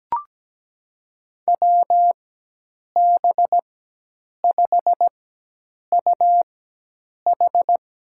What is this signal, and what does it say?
Morse code, heard as a keyed tone, about 17 wpm, spaced as for 8 wpm, 700 Hz. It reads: WB5UH
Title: USA Call Signs - 6 Characters - 4x Character Spacing - 17wpm